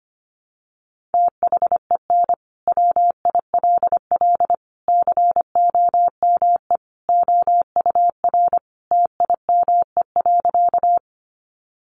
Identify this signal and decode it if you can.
Morse code, heard as a keyed tone, about 25 wpm, 700 Hz. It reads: THEN WILL COME OUR TIME.
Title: Sentences from Top 200 Words 25wpm